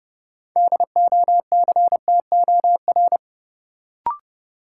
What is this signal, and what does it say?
Morse code, heard as a keyed tone, about 30 wpm, 700 Hz. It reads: DOCTOR
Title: Sentences from Top 1500 Words - New Word Drill Alt - 30wpm